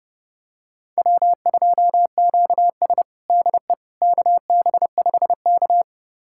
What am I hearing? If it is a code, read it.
Morse code, heard as a keyed tone, about 32 wpm, 700 Hz. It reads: W2QS DE KB5K
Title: USA Callsign DE Callsign - 4 Characters - 30wpm